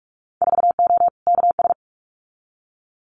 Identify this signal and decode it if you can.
Morse code, heard as a keyed tone, about 46 wpm, 690 Hz. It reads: 4O KS
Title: Winter Field Day Exchanges Encore 45wpm